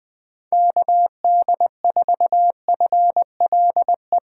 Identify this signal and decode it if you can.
Morse code, heard as a keyed tone, about 20 wpm, 700 Hz. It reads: KD4FLE
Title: USA Call Signs - 6 Characters - 3x Character Spacing - 20wpm